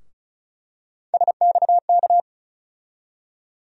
Morse code, heard as a keyed tone, about 36 wpm, 700 Hz. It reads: SXK
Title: USA Call Sign Postfixes - 3 Letters 35wpm